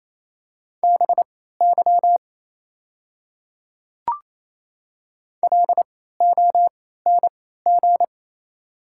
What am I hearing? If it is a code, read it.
Morse code, heard as a keyed tone, about 28 wpm, spaced as for 16 wpm, 700 Hz. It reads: BY LONG
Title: 3x Character Spacing - Top 100 Words - 28wpm